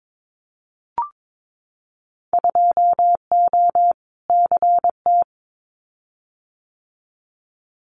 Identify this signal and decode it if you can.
Morse code, heard as a keyed tone, about 22 wpm, 700 Hz. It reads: 2O CT
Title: Winter Field Day Exchanges 22wpm